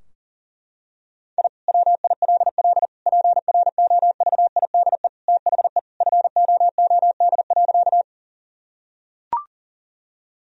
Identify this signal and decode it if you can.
Morse code, heard as a keyed tone, about 40 wpm, 700 Hz. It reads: I WILL PROVIDE THE FOOD.
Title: Sentences from Top 1000 Words - Review Part 3 - 40wpm